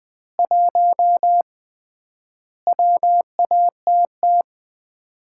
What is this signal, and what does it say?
Morse code, heard as a keyed tone, about 20 wpm, 700 Hz. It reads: 1 WATT
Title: Common QSO Phrases - 3x Word-Spacing - 20wpm